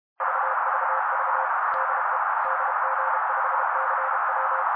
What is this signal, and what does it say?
Morse code, heard as a keyed tone, about 32 wpm, 530 Hz. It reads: VVV DE DG5CW
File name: daily / 31wpm noise